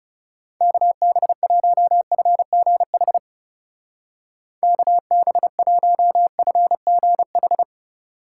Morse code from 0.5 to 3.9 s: KB1FGH

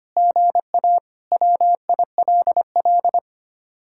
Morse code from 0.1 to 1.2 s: GA

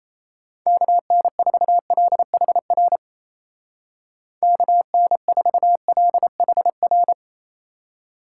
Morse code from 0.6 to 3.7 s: KN4LHR

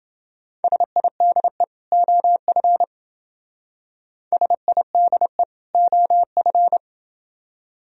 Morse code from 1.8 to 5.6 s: OF SIDE

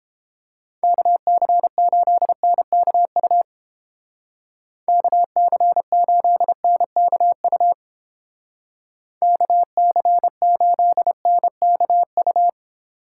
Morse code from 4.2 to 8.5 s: KC8NKU

